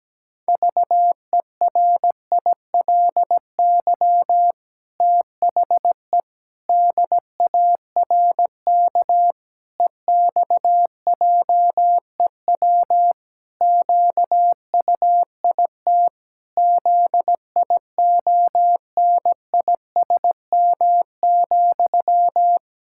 Morse code, heard as a keyed tone, about 17 wpm, 700 Hz. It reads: VERILY THE DARK EXJEW QUIT ZIONISM,
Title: Pangram Challenge v2 - 17wpm